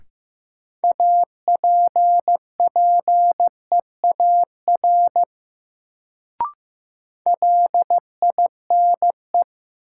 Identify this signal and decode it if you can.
Morse code, heard as a keyed tone, about 15 wpm, 700 Hz. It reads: APPEAR LINE
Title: Sentences from Top 1200 Words - New Word Drill Alt - 15wpm